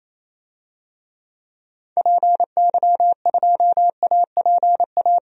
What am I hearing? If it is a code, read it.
Morse code, heard as a keyed tone, about 28 wpm, 700 Hz. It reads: PY2APA
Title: DXCC-100 Call Signs - 6 Characters - 28wpm